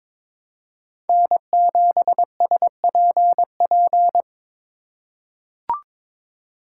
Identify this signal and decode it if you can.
Morse code, heard as a keyed tone, about 22 wpm, 700 Hz. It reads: N7SPP
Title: USA Call Signs - 5 Characters - 2x Character Spacing - 22wpm